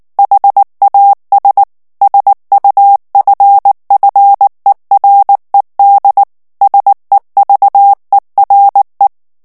Morse code, heard as a keyed tone, about 19 wpm, 800 Hz. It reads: HAS SUFFERED SEVERE
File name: ANGLAIS texte clair 2 vitesse 18 mots minute